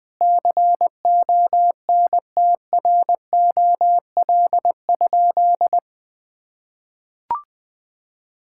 Morse code, heard as a keyed tone, about 20 wpm, 700 Hz. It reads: CONTROL?
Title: Sentences from Top 1000 Words - Review Part 5 - 20wpm